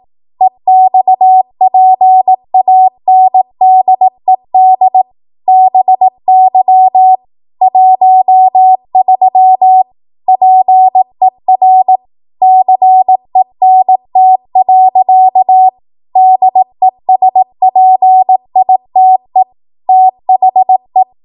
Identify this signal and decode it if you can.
Morse code, heard as a keyed tone, about 18 wpm, 750 Hz. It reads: EXPANDED BY 13 PER CENT. DESPITE THE
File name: Archive / 18 WPM / 18WPM